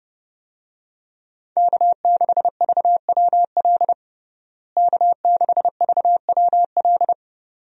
Morse code, heard as a keyed tone, about 32 wpm, 700 Hz. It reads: K6VWL K6VWL
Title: USA Call Signs - 5 Characters - Repeated 2x - 30wpm